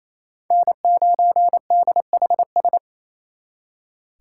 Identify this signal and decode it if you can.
Morse code, heard as a keyed tone, about 34 wpm, 700 Hz. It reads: N9DHS
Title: Speed Racing - USA Call Signs 28wpm